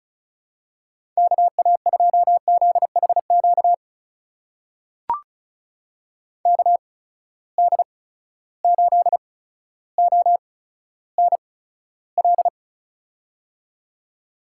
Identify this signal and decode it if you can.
Morse code, heard as a keyed tone, about 36 wpm, spaced as for 15 wpm, 700 Hz. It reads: KA2ZHQ KD8ONL
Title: USA Call Signs - 6 Characters - 8x Character Spacing - 35wpm